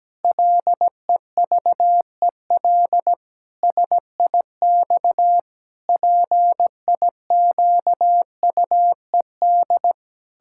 Morse code, heard as a keyed tone, about 17 wpm, 700 Hz. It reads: LEVEL SIX PIQUED
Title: Pangram Challenge v2 - 17wpm